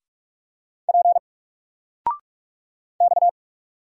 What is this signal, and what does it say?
Morse code, heard as a keyed tone, about 44 wpm, 700 Hz. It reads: P X
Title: Single Letter-Number Alt 45wpm